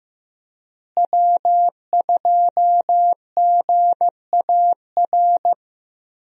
Morse code, heard as a keyed tone, about 15 wpm, 700 Hz. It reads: W2GAR
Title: USA Call Signs - 5 Characters - Repeated 2x - 15wpm